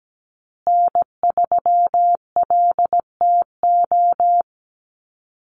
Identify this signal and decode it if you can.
Morse code, heard as a keyed tone, about 17 wpm, 700 Hz. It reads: N3LTO
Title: USA Call Signs Repeated 3x 17wpm